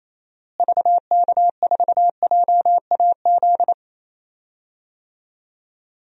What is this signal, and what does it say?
Morse code, heard as a keyed tone, about 28 wpm, 700 Hz. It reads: VK4JAZ
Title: DXCC-100 Call Signs - 6 Characters - Repeated 3x - 28wpm